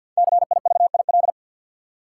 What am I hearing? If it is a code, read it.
Morse code, heard as a keyed tone, about 52 wpm, 700 Hz. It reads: CIVIL
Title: Five Letter Words - 50wpm